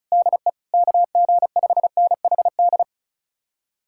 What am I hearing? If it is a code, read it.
Morse code, heard as a keyed tone, about 33 wpm, 700 Hz. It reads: DE KG5NHD
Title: Speed-Racing Common QSO Phrases - 35wpm